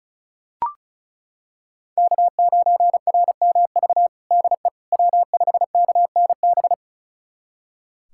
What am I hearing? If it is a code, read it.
Morse code, heard as a keyed tone, about 34 wpm, 700 Hz. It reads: K9RMV DE W5KNB
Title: USA Callsign DE Callsign - 5 Characters - 35wpm